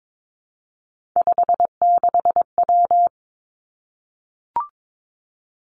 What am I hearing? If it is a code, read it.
Morse code, heard as a keyed tone, about 22 wpm, 700 Hz. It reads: 56W
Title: Three Challenging Letter-Number Combinations 22wpm